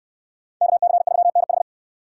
Morse code, heard as a keyed tone, about 68 wpm, 700 Hz. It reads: KC3NH